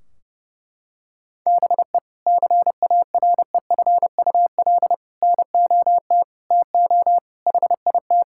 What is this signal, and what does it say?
Morse code, heard as a keyed tone, about 32 wpm, 700 Hz. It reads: BE CAREFUL NOT TO HIT
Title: Sentences from Top 1600 Words 30wpm